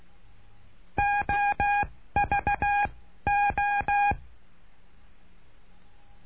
Morse code, heard as a keyed tone, about 16 wpm, 820 Hz. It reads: O V O
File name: cursos / CURSO DE TELEGRAFIA / Telegrafia - Trilha 09